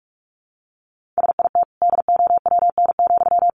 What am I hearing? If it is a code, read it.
Morse code, heard as a keyed tone, about 44 wpm, 700 Hz. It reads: SIT DOWN,
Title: Sentences from Top 500 Words 45wpm